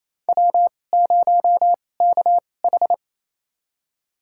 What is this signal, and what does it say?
Morse code, heard as a keyed tone, about 27 wpm, spaced as for 20 wpm, 700 Hz. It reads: W0KH
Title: USA Call Signs - 2x Character Spacing - 28wpm